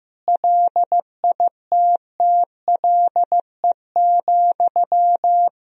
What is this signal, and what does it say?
Morse code, heard as a keyed tone, about 15 wpm, 700 Hz. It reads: LITTLE,
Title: Sentences from Top 400 Words and Abbreviations 15wpm